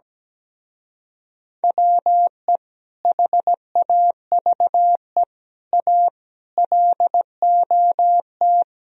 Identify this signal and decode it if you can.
Morse code, heard as a keyed tone, about 17 wpm, 700 Hz. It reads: WE HAVE A LOT
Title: Speed-Racing - Sets of 5 Words 17wpm